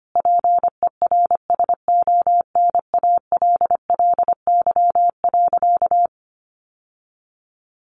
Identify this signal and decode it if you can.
Morse code, heard as a keyed tone, about 25 wpm, 700 Hz. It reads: PERSONALLY.
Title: Sentences from Top 1400 Words 25wpm